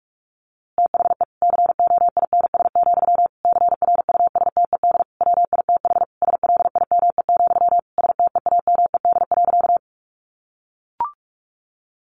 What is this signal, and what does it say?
Morse code, heard as a keyed tone, about 46 wpm, 700 Hz. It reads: THE COINS, CRUSTED WITH SLIME, STEAMED.
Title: A Sound Of Thunder 45wpm